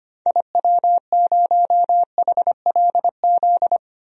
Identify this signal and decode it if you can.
Morse code, heard as a keyed tone, about 25 wpm, 700 Hz. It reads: IW0HLZ
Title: DXCC-100 Call Signs - 6 Characters - 25wpm